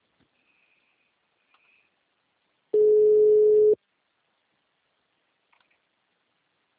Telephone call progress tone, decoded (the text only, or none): ringback tone